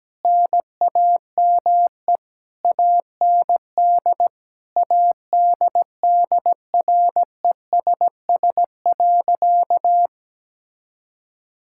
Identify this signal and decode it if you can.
Morse code, heard as a keyed tone, about 17 wpm, 700 Hz. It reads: NAME AND ADDRESS.
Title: Sentences from Top 1000 Words - Review Part 6 - 17wpm